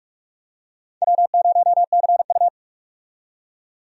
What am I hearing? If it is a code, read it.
Morse code, heard as a keyed tone, about 46 wpm, 700 Hz. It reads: W0CU
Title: USA Call Signs - 4 Characters - 45wpm